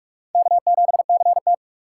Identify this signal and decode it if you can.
Morse code, heard as a keyed tone, about 42 wpm, 700 Hz. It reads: K7CT